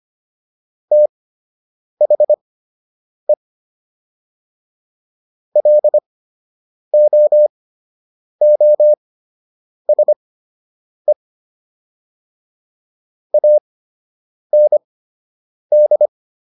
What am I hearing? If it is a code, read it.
Morse code, heard as a keyed tone, about 25 wpm, spaced as for 8 wpm, 600 Hz. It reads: THE LOOSE AND